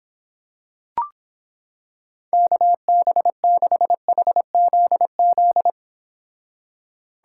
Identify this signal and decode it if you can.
Morse code, heard as a keyed tone, about 26 wpm, 700 Hz. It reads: KB6HZZ